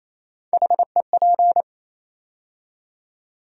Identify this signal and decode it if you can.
Morse code, heard as a keyed tone, about 28 wpm, 700 Hz. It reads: HEP
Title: USA Call Sign Postfixes - 3 Letters 28wpm